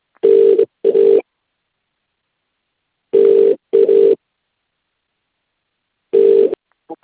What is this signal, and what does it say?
Ringback tone (UK), 400 + 450 Hz.